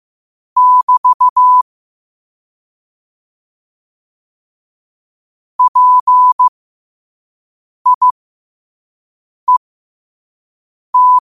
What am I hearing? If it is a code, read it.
Morse code, heard as a keyed tone, about 15 wpm, spaced as for 5 wpm, 1000 Hz. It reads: <BT> PIET
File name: qotdfiles / 5wpm